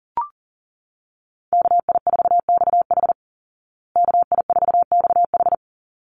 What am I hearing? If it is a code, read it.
Morse code, heard as a keyed tone, about 40 wpm, 700 Hz. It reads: KI4XH KI4XH
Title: Call Signs Repeated 3x - 40wpm